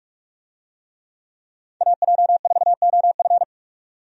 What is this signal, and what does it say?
Morse code, heard as a keyed tone, about 46 wpm, 700 Hz. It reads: AJ4OF
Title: USA Call Signs - 5 Characters - 45wpm